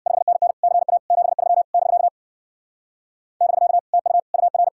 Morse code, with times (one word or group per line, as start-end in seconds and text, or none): 0.1-0.5 5NN
0.6-1.0 CA
1.1-1.6 73
1.7-2.1 <BK>
3.4-3.8 <BK>
3.9-4.2 TU
4.3-4.7 UR